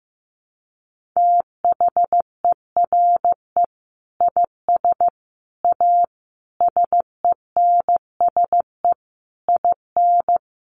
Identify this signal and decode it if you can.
Morse code, heard as a keyed tone, about 15 wpm, 700 Hz. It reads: THERE IS A SENSE IN
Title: Sentences from Top 1500 Words 15wpm